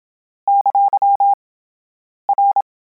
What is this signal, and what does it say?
Morse code, heard as a keyed tone, about 27 wpm, 800 Hz. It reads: ! R